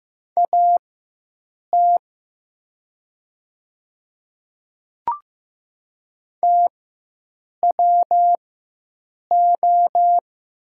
Morse code, heard as a keyed tone, about 15 wpm, spaced as for 7 wpm, 700 Hz. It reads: AT TWO
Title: 4x Character Spacing - Top 100 Words - 15wpm